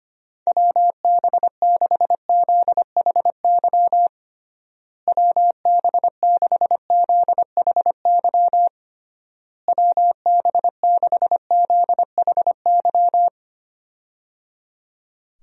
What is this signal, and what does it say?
Morse code, heard as a keyed tone, about 25 wpm, 700 Hz. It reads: WB6ZHY WB6ZHY WB6ZHY
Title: USA Call Signs - 6 Characters - Repeated 3x - 25wpm